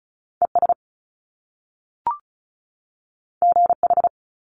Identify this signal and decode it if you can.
Morse code, heard as a keyed tone, about 36 wpm, 700 Hz. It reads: ES GH
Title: Top Two Letter Combinations in Words 35wpm